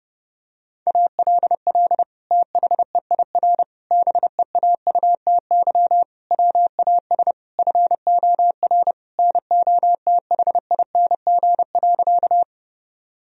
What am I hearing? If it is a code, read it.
Morse code, heard as a keyed tone, about 32 wpm, 700 Hz. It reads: ALL THEIR BEAUTY WAS FOR NOTHING.
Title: Sentences from Top 1000 Words - Review Part 2 - 30wpm